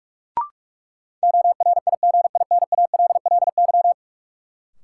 Morse code, heard as a keyed tone, about 42 wpm, 700 Hz. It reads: ORIGINALLY
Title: Sentences from Top 1500 Words - New Word Drill - 45wpm